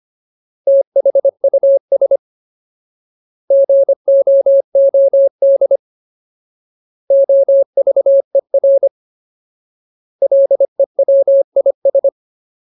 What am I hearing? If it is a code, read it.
Morse code, heard as a keyed tone, about 25 wpm, 550 Hz. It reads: THUS GOOD OVER LEWIS
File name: Ultimate Word List / 25WPM Ultimate